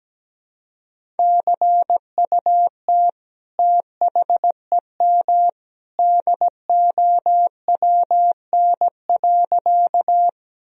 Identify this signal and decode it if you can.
Morse code, heard as a keyed tone, about 17 wpm, 700 Hz. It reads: CUT THEM DOWN.
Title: Sentences from Top 1000 Words - Review Part 1 - 17wpm